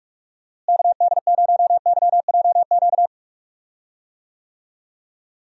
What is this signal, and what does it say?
Morse code, heard as a keyed tone, about 44 wpm, 700 Hz. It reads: KD0YJQ
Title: USA Call Signs - 6 Characters - Repeated 3x - 45wpm